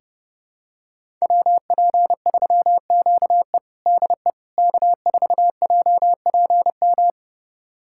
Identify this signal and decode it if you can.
Morse code, heard as a keyed tone, about 32 wpm, 700 Hz. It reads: WP3QE DE K4JPM
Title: USA Callsign DE Callsign - 5 Characters - 30wpm